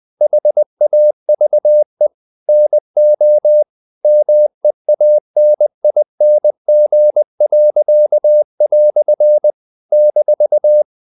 Morse code, heard as a keyed tone, about 20 wpm, 600 Hz. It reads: HAVE NO MEANING." -
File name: CW / Practice / EnglishQuotations0320